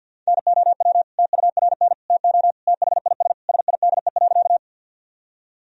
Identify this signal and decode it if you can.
Morse code, heard as a keyed tone, about 47 wpm, 700 Hz. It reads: NOW TURN TO THIS SIDE.
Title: Sentences from Top 1000 Words - Review Part 1 - 50wpm